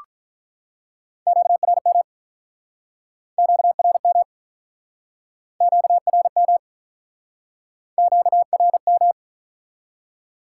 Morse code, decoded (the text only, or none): QRM QRM QRM QRM